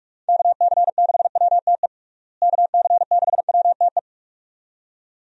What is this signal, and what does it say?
Morse code, heard as a keyed tone, about 46 wpm, 700 Hz. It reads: KC6WTE KC6WTE
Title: USA Call Signs - 6 Characters - Repeated 2x - 45wpm